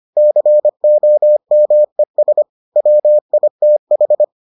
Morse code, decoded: COMES WITH

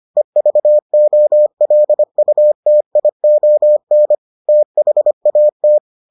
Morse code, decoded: EVOLUTION THAT

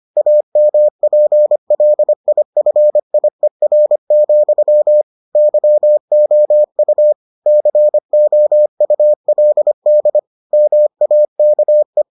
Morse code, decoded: AMPLIFIER, YOU COULD MAKE

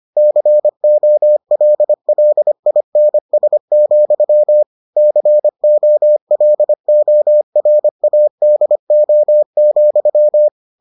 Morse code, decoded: COLLINS, COLORADO,

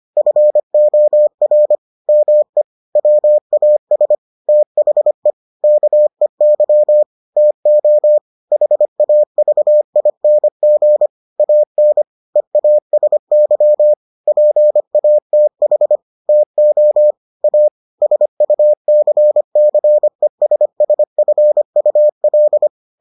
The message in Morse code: FOR ME WAS THE KEY TO HAVING AN EASY PATH TO A SUCCESSFUL

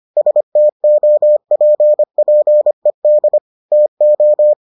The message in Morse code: STOPPED TO

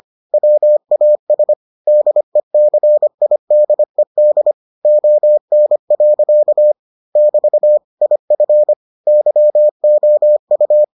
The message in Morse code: WAS DECIDED ON. <BT> IF YOU

